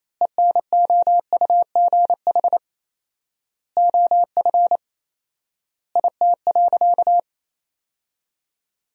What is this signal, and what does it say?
Morse code, heard as a keyed tone, about 28 wpm, 700 Hz. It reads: ENOUGH OF IT.
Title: 4x Word-Spacing - Sentences from Top 400 Words - 28wpm